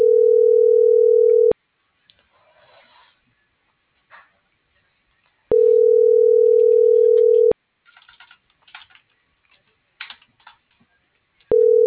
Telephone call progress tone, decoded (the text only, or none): ringback tone